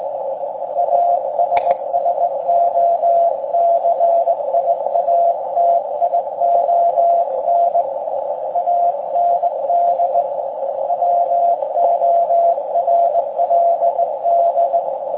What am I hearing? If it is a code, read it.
Morse code, heard as a keyed tone, about 19 wpm, 690 Hz, with noise. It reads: ASSOCIATION AND WORLD
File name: forum / im / aq grimeton